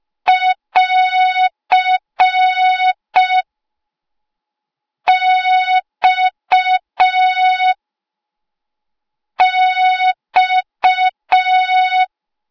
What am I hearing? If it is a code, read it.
Morse code, heard as a keyed tone, about 5 wpm, 3700 Hz. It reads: <AR> X X